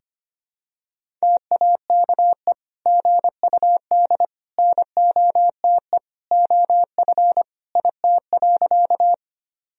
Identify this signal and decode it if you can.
Morse code, heard as a keyed tone, about 25 wpm, 700 Hz. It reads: TAKE GUD NOTE OF IT.
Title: Sentences from Top 500 Words and Abbreviations 25wpm